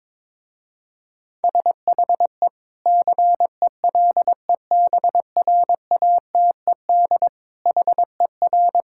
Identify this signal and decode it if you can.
Morse code, heard as a keyed tone, about 23 wpm, 700 Hz. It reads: SHE CELEBRATED HER
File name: Difficult Sentences with Hints - 22wpm